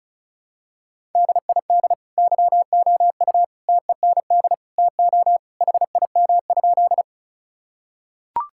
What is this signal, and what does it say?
Morse code, heard as a keyed tone, about 36 wpm, 700 Hz. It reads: DID YOU TEND TO HIM?
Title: Sentences from Top 1000 Words - Review Part 5 - 35wpm